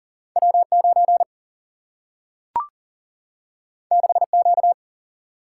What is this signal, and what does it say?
Morse code, heard as a keyed tone, about 39 wpm, 700 Hz. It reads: W9 6Q
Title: Single-Letter Single-Number 40wpm